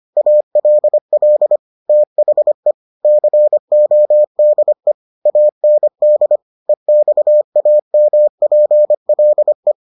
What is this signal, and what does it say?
Morse code, heard as a keyed tone, about 25 wpm, 600 Hz. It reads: ALL THE CODE AND EXAMPLE